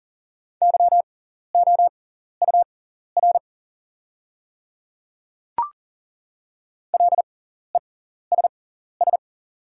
Morse code, heard as a keyed tone, about 37 wpm, spaced as for 14 wpm, 690 Hz. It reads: YOUR LESS
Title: 6x Character Spacing - Top 500 Words - 40wpm